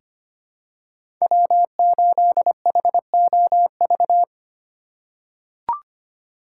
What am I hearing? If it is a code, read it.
Morse code, heard as a keyed tone, about 25 wpm, 700 Hz. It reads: W8HOV
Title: USA Call Signs - 5 Characters - 3x Character Spacing - 25wpm